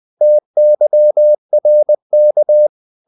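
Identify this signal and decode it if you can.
Morse code, heard as a keyed tone, about 20 wpm, 600 Hz. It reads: TYRK
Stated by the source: Morse code practice of short Danish words at 20 wpm, from CW med Gnister.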